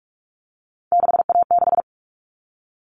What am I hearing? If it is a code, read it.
Morse code, heard as a keyed tone, about 44 wpm, 700 Hz. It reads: 6A6
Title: All Cut Numbers - 3-Digit 45wpm